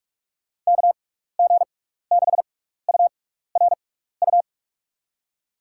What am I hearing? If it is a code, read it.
Morse code, heard as a keyed tone, about 46 wpm, spaced as for 15 wpm, 700 Hz. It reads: KG6URU
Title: USA Call Signs - 6x Character Spacing - 45wpm